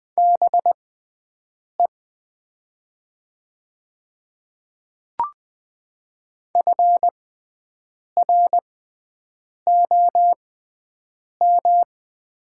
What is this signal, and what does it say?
Morse code, heard as a keyed tone, about 20 wpm, spaced as for 7 wpm, 700 Hz. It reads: BE FROM
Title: 6x Character Spacing - Top 200 Words - 20wpm